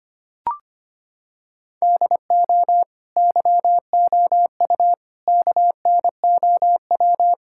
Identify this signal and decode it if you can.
Morse code, heard as a keyed tone, about 25 wpm, 700 Hz. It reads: DO YOU KNOW
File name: practice / 25wpm / Sentences with 2-Word Phrases - 25wpm